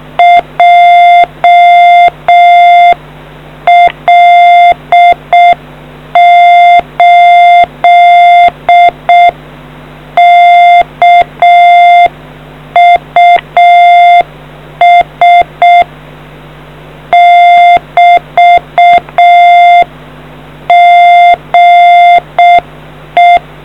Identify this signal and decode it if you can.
Morse code, heard as a keyed tone, about 6 wpm, 700 Hz. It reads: JL8KUS <BT> GE